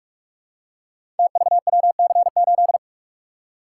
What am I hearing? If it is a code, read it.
Morse code, heard as a keyed tone, about 44 wpm, 700 Hz. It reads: TVWC8
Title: Code Groups - 5 Letter:Numbers - 45wpm